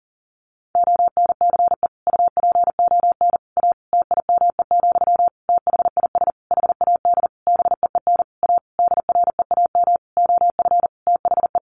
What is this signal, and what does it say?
Morse code, heard as a keyed tone, about 40 wpm, 700 Hz. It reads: ONCE UPON A TIME, THIS HAD BEEN A DREAM OF THE
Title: Sentences from Top 1500 Words 40wpm